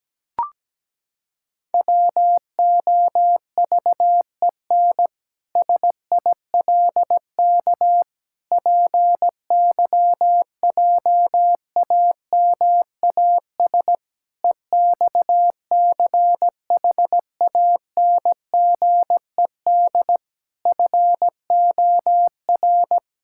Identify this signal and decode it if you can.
Morse code, heard as a keyed tone, about 17 wpm, 700 Hz. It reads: WOVEN SILK PYJAMAS EXCHANGED FOR
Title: Pangram Challenge v2 - 17wpm